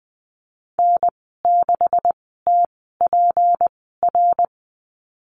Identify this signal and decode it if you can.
Morse code, heard as a keyed tone, about 20 wpm, spaced as for 15 wpm, 700 Hz. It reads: N6TPR
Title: 193 Lesson 31g - Letter v - Call Signs 20wpm